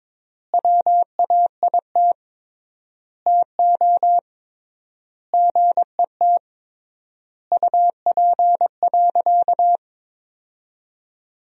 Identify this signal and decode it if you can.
Morse code, heard as a keyed tone, about 22 wpm, 700 Hz. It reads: WAIT TO GET UP.
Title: 3x Word-Spacing - Sentences from Top 500 Words - 22wpm